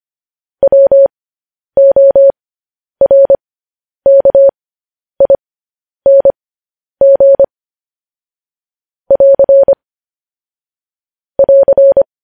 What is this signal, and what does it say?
Morse code, heard as a keyed tone, about 25 wpm, spaced as for 10 wpm, 550 Hz. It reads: WORKING <AR> <AR>
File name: podcast / News headlines-1x-25x10